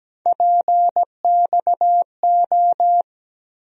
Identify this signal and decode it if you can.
Morse code, heard as a keyed tone, about 17 wpm, 700 Hz. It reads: PXO
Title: USA Call Sign Postfixes - 3 Letters 17wpm